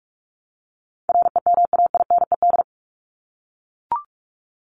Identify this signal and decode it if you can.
Morse code, heard as a keyed tone, about 44 wpm, 700 Hz. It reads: REMAINED
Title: ED Suffix - Top 100 Words with Base Words 45wpm